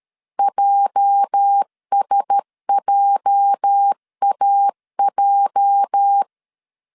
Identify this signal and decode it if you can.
Morse code, heard as a keyed tone, about 13 wpm, 800 Hz. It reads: JSJAJ
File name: LEZIONI / 8 YJ45 / 8 YJ 12wpm 2235